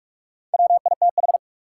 Morse code, heard as a keyed tone, about 44 wpm, 700 Hz. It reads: WITH
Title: Top 180 QSO Elements 45wpm